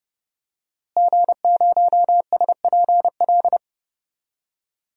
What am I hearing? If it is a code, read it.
Morse code, heard as a keyed tone, about 32 wpm, 700 Hz. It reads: G0SPL
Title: DXCC-100 Call Signs - 5 Characters - Repeated 3x - 30wpm